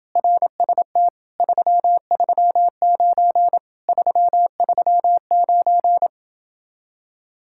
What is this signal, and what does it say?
Morse code, heard as a keyed tone, about 29 wpm, 700 Hz. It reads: RST 339 339